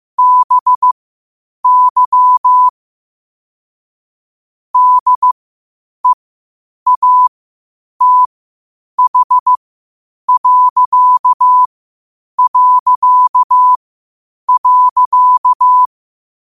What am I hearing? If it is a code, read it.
Morse code, heard as a keyed tone, about 15 wpm, spaced as for 8 wpm, 1000 Hz. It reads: BY DEATH...
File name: qotdfiles / 8wpm